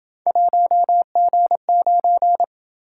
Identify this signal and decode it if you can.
Morse code, heard as a keyed tone, about 27 wpm, 700 Hz. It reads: 1G9